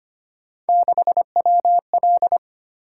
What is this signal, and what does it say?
Morse code, heard as a keyed tone, about 26 wpm, 700 Hz. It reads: 6WL
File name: Three Letter-Number Combinations - 25wpm